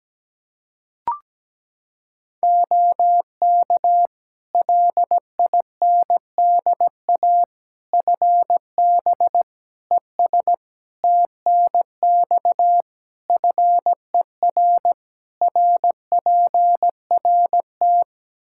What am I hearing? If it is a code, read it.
Morse code, heard as a keyed tone, about 17 wpm, 700 Hz. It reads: OK LINDA FB ES TNX FER RPRT
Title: Practice QSO - Exchange 2 - Info - 17wpm